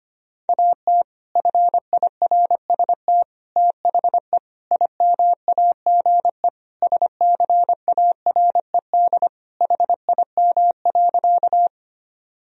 Morse code, heard as a keyed tone, about 25 wpm, 700 Hz. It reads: AT FIRST THE IMAGE SCARED HIM.
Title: Sentences from Top 1000 Words - Review Part 4 - 25wpm